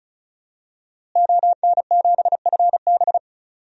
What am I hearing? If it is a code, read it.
Morse code, heard as a keyed tone, about 36 wpm, 700 Hz. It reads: ON7FB
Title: DXCC-100 Call Signs - 5 Characters - Repeated 3x - 35wpm